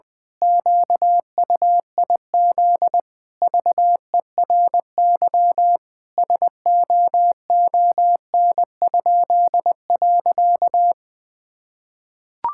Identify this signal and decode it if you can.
Morse code, heard as a keyed tone, about 20 wpm, 700 Hz. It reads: QUIZ VERY SOON?.
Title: Pangram Challenge v2 - 20wpm